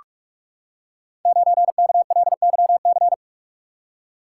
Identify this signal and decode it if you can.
Morse code, heard as a keyed tone, about 44 wpm, 700 Hz. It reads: 9KLYC
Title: Code Groups - 5 Letter:Numbers - 45wpm